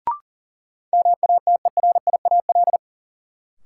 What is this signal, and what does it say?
Morse code, heard as a keyed tone, about 40 wpm, 700 Hz. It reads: MATERIAL
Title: Sentences from Top 1000 Words - Review Part 5 - New Word Drill - 40wpm